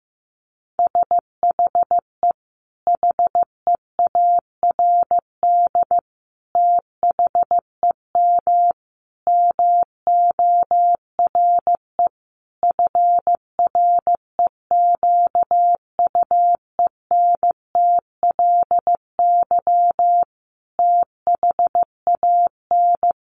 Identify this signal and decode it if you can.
Morse code, heard as a keyed tone, about 15 wpm, 700 Hz. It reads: SHE HEARD THEM MORE FREQUENTLY THAN
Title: Sentences from Top 1300 Words 15wpm